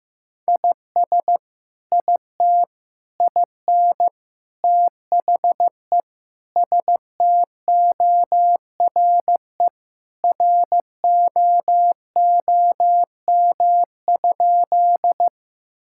Morse code, heard as a keyed tone, about 15 wpm, 700 Hz. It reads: IS IT IN THE STORE ROOM?
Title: Sentences from Top 1000 Words - Review Part 5 - 15wpm